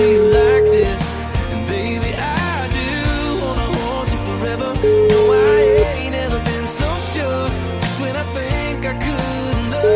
Ringback tone (Europe, 425 Hz), with noise.